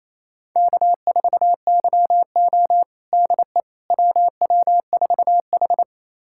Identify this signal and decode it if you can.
Morse code, heard as a keyed tone, about 28 wpm, 700 Hz. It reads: K4YO DE WW4H
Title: USA Callsign DE Callsign - 4 Characters - 28wpm